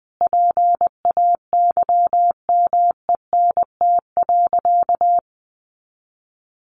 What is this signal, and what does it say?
Morse code, heard as a keyed tone, about 20 wpm, 700 Hz. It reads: PAYMENT.
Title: Sentences from Top 1200 Words 20wpm